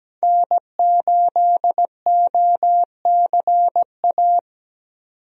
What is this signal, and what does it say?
Morse code, heard as a keyed tone, about 17 wpm, 700 Hz. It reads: N8OCA
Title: USA Call Signs - 5 Characters - Repeated 3x - 17wpm